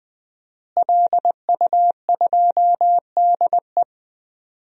Morse code, heard as a keyed tone, about 21 wpm, 700 Hz. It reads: LU2DE
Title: DXCC-100 Call Signs - 5 Characters - Repeated 3x - 20wpm